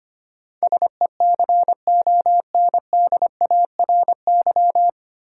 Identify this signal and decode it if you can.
Morse code, heard as a keyed tone, about 25 wpm, 690 Hz. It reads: SECONDARY
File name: practice / 25wpm / 9 Letter Words - First 3 Letters Given - 25wpm